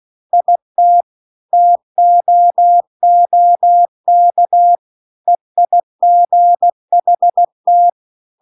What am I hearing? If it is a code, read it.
Morse code, heard as a keyed tone, about 16 wpm, 700 Hz. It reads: IT TOOK EIGHT